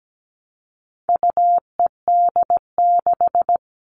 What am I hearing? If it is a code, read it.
Morse code, heard as a keyed tone, about 17 wpm, 700 Hz. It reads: UED6
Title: All Cut Numbers - 4-Digit 17wpm